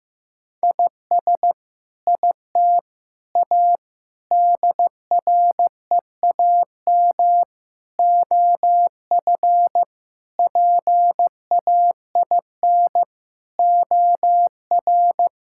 Morse code, heard as a keyed tone, about 16 wpm, 700 Hz. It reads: IS IT A DREAM OF PAIN OR
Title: Sentences from Top 1600 Words 15wpm